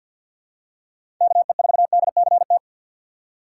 Morse code, heard as a keyed tone, about 52 wpm, 700 Hz. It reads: KE4DCT
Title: USA Call Signs - 6 Characters - 6x Character Spacing - 50wpm